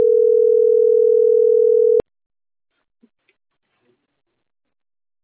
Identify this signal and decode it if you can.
Ringback tone (North America), 440 + 480 Hz.